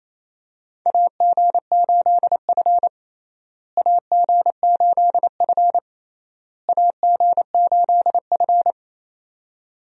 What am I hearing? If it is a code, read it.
Morse code, heard as a keyed tone, about 28 wpm, 700 Hz. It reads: AG8F AG8F AG8F
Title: USA Call Signs - 4 Characters - Repeated 3x - 28wpm